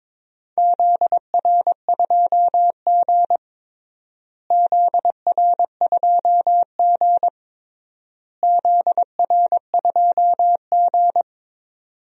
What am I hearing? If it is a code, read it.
Morse code, heard as a keyed tone, about 22 wpm, 700 Hz. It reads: ZR2G ZR2G ZR2G
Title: DXCC-100 Call Signs - 4 Characters - Repeated 3x - 22wpm